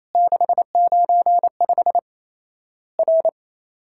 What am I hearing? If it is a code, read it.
Morse code, heard as a keyed tone, about 28 wpm, 700 Hz. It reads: 695 R
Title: Worked All Europe DX Contest - QTC Exchange - 28wpm